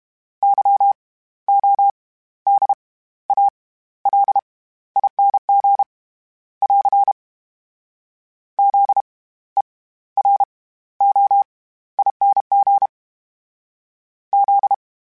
Morse code, heard as a keyed tone, about 32 wpm, 800 Hz. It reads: Y O D A L ING <AR> Z E R O ING Z
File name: Special INGs / Special INGs-mp3s / SlowToFaster ing @30